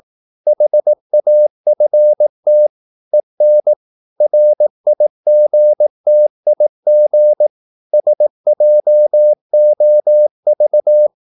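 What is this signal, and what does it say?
Morse code, heard as a keyed tone, about 18 wpm, 600 Hz. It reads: HAFT EN RIGTIG SJOV
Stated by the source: Slow (QRS) Morse practice at 18 wpm, CW med Gnister.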